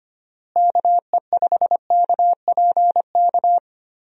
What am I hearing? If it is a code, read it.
Morse code, heard as a keyed tone, about 25 wpm, 700 Hz. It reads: KE5KPK
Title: USA Call Signs - 6 Characters - Repeated 2x - 25wpm